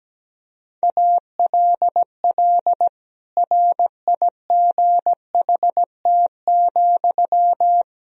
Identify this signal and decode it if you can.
Morse code, heard as a keyed tone, about 17 wpm, 700 Hz. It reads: ALL RIGHT,
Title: Sentences from Top 1000 Words - Review Part 4 - 17wpm